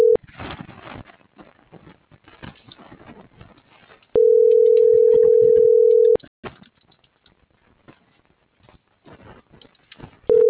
Ringback tone (North America), 440 + 480 Hz.
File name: CENTRAL SAGOT / out danae.bsagot - .62422